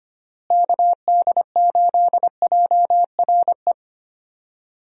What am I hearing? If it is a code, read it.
Morse code, heard as a keyed tone, about 25 wpm, 700 Hz. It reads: KD8JRE
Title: USA Call Signs - 6 Characters - Repeated 3x - 25wpm